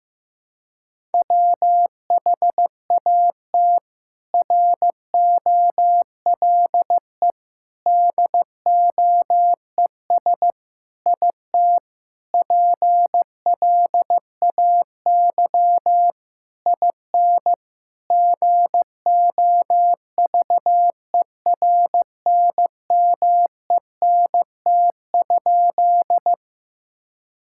Morse code, heard as a keyed tone, about 15 wpm, 700 Hz. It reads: WHAT ROLE DOES IT PLAY IN GOVERNMENT?
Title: Sentences from Top 1000 Words - Review Part 6 - 15wpm